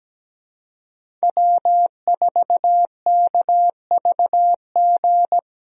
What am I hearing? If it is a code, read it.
Morse code, heard as a keyed tone, about 17 wpm, 700 Hz. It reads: W4KVG
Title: USA Call Signs - 5 Characters - 17wpm